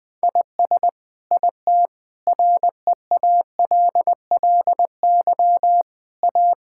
Morse code, heard as a keyed tone, about 20 wpm, 700 Hz. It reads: IS IT REALLY A
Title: Sentences from Top 1000 Words - Review Part 5 - 20wpm